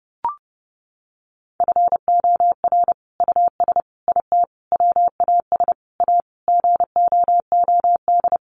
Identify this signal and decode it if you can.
Morse code, heard as a keyed tone, about 30 wpm, 700 Hz. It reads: FOR US IT WAS A GOOD
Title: Sentences from Top 1200 Words 30wpm